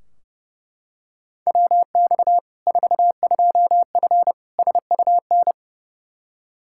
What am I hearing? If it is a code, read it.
Morse code, heard as a keyed tone, about 30 wpm, 700 Hz. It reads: WX 42F SUN
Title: K1USN SST QSO Practice 30wpm